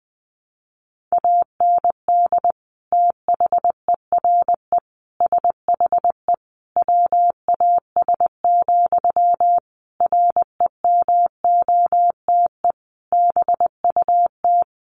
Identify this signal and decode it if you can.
Morse code, heard as a keyed tone, about 20 wpm, 700 Hz. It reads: AND THERE SHE WAS, REMOTE BUT
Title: Sentences from Top 1300 Words 20wpm